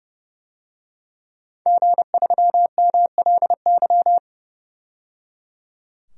Morse code, heard as a keyed tone, about 30 wpm, 700 Hz. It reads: G3MLY
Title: DXCC-100 Call Signs - 5 Characters - 30wpm